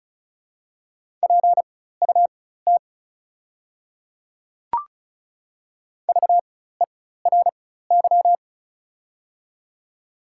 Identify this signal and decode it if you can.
Morse code, heard as a keyed tone, about 36 wpm, spaced as for 16 wpm, 700 Hz. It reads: PUT VERY
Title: 4x Character Spacing - Top 200 Words - 35wpm